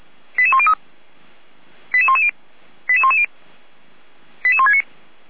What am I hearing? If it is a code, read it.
Five-tone selective call (ZVEI), heard as quick runs of five tones, tones of about 70 ms. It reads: 88103 88110 88110 88169